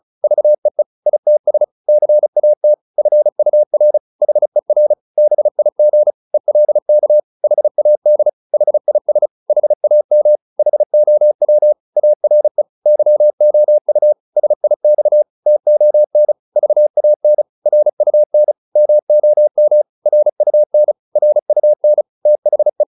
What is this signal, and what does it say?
Morse code, heard as a keyed tone, about 35 wpm, 600 Hz. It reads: VEE ITS CAT FUR HER BIG ELK HAD HIS HAM HOW ARE YOU SIX TON VAN RUN MOM RUN RUN THE